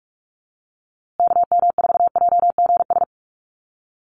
Morse code, heard as a keyed tone, about 44 wpm, 700 Hz. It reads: KM4JGS
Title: USA Call Signs Repeated 3x 45wpm